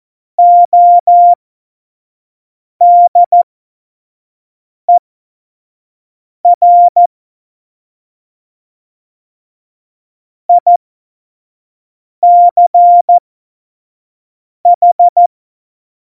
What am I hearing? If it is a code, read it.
Morse code, heard as a keyed tone, about 14 wpm, spaced as for 5 wpm, 700 Hz. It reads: ODER ICH
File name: Morsekurs / Witze / Witz 07 April 2020, 25 F70